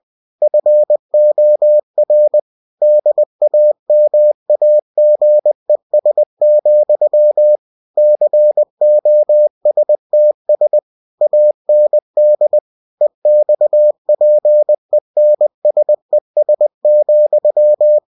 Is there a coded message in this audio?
Yes: Morse code from 0.4 to 18.0 s, heard as a keyed tone, about 20 wpm, 600 Hz. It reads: FOR DAMAGES, COSTS AND EXPENSES,